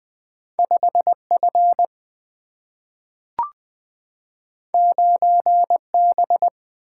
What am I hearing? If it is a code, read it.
Morse code, heard as a keyed tone, about 20 wpm, 700 Hz. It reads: 5 F 9 B
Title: Single-Letter Single-Number 20wpm